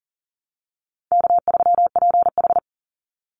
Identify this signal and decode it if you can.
Morse code, heard as a keyed tone, about 40 wpm, 700 Hz. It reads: K3PH
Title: Call Signs Repeated 2x - 40wpm